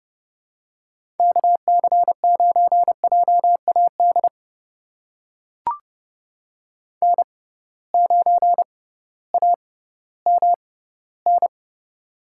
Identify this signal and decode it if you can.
Morse code, heard as a keyed tone, about 30 wpm, spaced as for 16 wpm, 700 Hz. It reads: KC9JAD N9AMN
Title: USA Call Signs - 6x Character Spacing - 30wpm